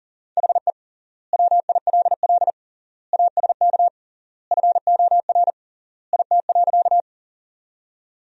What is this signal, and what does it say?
Morse code, heard as a keyed tone, about 40 wpm, 700 Hz. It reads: HE WILL ASK FOR IT.
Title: 3x Word-Spacing - Sentences from Top 200 Words - 40wpm